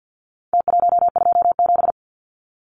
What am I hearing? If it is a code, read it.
Morse code, heard as a keyed tone, about 49 wpm, 700 Hz. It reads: T127
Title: Numbers - Contest Style - 4-Digit 50wpm